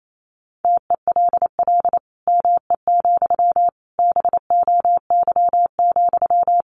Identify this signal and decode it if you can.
Morse code, heard as a keyed tone, about 28 wpm, 700 Hz. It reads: TELL ME, BOY,
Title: Sentences from Top 1500 Words 28wpm